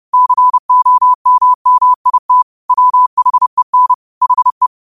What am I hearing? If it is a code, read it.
Morse code, heard as a keyed tone, about 30 wpm, 1000 Hz. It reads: COMMIT WHEN HE
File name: qotdfiles / 30wpm